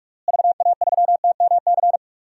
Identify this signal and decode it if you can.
Morse code, heard as a keyed tone, about 46 wpm, 700 Hz. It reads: VA3TMC
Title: DXCC-100 Call Signs - 6 Characters - Repeated 2x - 45wpm